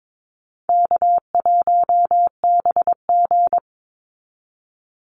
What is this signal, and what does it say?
Morse code, heard as a keyed tone, about 22 wpm, 700 Hz. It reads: K1BG
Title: Easy Call Signs - 22wpm